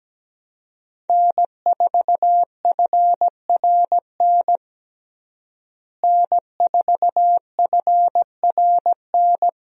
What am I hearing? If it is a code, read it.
Morse code, heard as a keyed tone, about 17 wpm, 700 Hz. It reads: N4FRN N4FRN
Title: USA Call Signs - 5 Characters - Repeated 3x - 17wpm